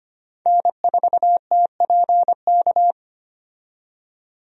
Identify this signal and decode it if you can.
Morse code, heard as a keyed tone, about 25 wpm, 700 Hz. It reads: N4TPK
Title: USA Call Signs - 5 Characters - 25wpm